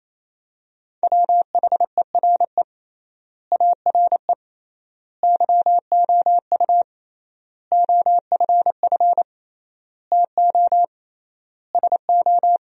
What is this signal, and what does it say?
Morse code, heard as a keyed tone, about 28 wpm, 700 Hz. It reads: WHERE ARE YOU OFF TO SO
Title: 3x Word-Spacing - Sentences from Top 400 Words - 28wpm